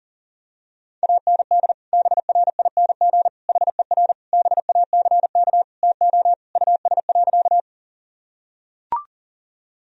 Morse code, heard as a keyed tone, about 37 wpm, 700 Hz. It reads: AND BRING HER BACK TO US.
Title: Sentences from Top 1000 Words - Review Part 3 - 40wpm